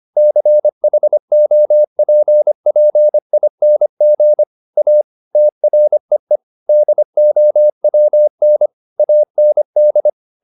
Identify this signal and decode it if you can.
Morse code, heard as a keyed tone, about 25 wpm, 600 Hz. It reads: CHOPPING A TREE DOWN AND
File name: the old man and the sea part 1 25wpm